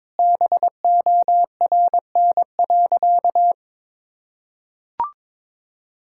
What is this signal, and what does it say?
Morse code, heard as a keyed tone, about 22 wpm, 700 Hz. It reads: BORN.
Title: Sentences from Top 1600 Words 22wpm